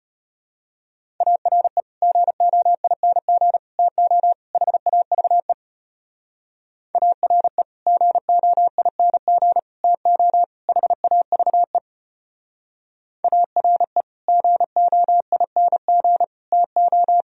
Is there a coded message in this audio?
Yes: Morse code from 1.2 to 17.2 s, heard as a keyed tone, about 32 wpm, 700 Hz. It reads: ARE GOING TO HAVE ARE GOING TO HAVE ARE GOING TO